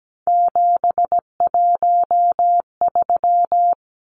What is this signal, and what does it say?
Morse code, heard as a keyed tone, about 17 wpm, 700 Hz. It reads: 713
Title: Numbers - 3-Digit 17wpm